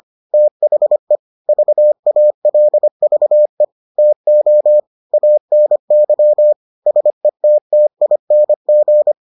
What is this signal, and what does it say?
Morse code, heard as a keyed tone, about 25 wpm, 600 Hz. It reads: THE VALVE TO ANY SETTING